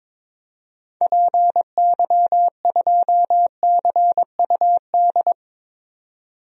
Morse code, heard as a keyed tone, about 22 wpm, 700 Hz. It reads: PY2CUD
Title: DXCC-100 Call Signs - 6 Characters - 22wpm